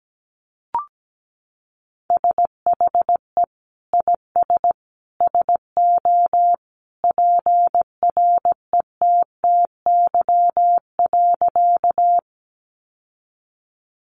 Morse code, heard as a keyed tone, about 17 wpm, 700 Hz. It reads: SHE IS SO PRETTY.
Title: Sentences from Top 1000 Words 17wpm